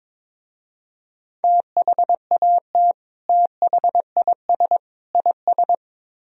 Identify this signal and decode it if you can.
Morse code, heard as a keyed tone, about 24 wpm, 700 Hz. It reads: THAT THIS IS
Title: Speed-Racing - Sets of 3 Words 22wpm